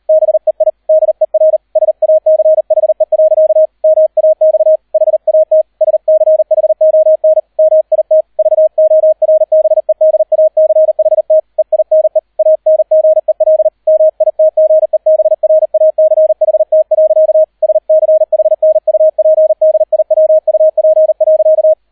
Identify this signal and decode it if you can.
Morse code, heard as a keyed tone, about 38 wpm, 630 Hz, with radio noise. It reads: BEI DER SACHE. MAX HAT SCHON MIT VORBEDACHT EINE ANGEL MITGEBRACHT. SCHNUPDIWUP.